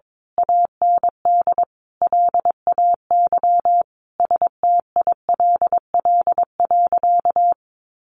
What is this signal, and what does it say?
Morse code, heard as a keyed tone, about 22 wpm, 700 Hz. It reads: AND LAY STILL.
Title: A Sound Of Thunder 22wpm